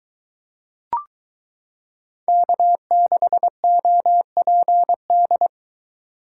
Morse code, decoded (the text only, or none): K6OPD